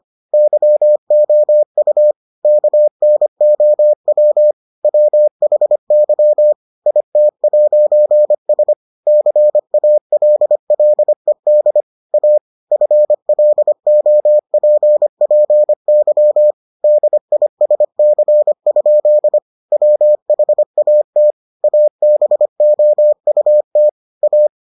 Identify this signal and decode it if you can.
Morse code, heard as a keyed tone, about 25 wpm, 600 Hz. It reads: YOU KNOW WHY IT'S CALLED A FLOPPY DISC? WHAT ABOUT A